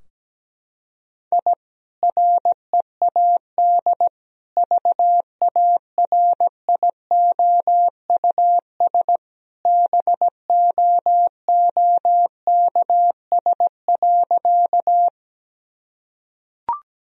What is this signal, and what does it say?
Morse code, heard as a keyed tone, about 17 wpm, 700 Hz. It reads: I READ VARIOUS BOOKS.
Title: Sentences from Top 1000 Words - Review Part 4 - 17wpm